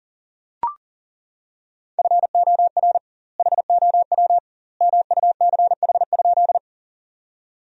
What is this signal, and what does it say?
Morse code, heard as a keyed tone, about 40 wpm, 700 Hz. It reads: FOR HOW MUCH?
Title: 2x Word-Spacing - Sentences from Top 200 Words - 40wpm